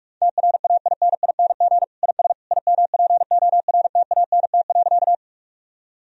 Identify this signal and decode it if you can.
Morse code, heard as a keyed tone, about 41 wpm, 700 Hz. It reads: TRAINING IS IMPORTANT.
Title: Sentences from Top 1000 Words - Review Part 4 - 45wpm